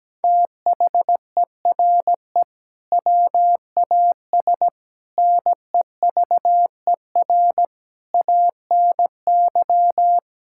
Morse code, heard as a keyed tone, about 17 wpm, 700 Hz. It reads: THERE WAS NEVER ANY
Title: Sentences from Top 1000 Words - Review Part 6 - 17wpm